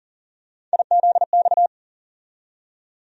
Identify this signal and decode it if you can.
Morse code, heard as a keyed tone, about 40 wpm, 700 Hz. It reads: IZX
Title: USA Call Sign Postfixes - 3 Letters 40wpm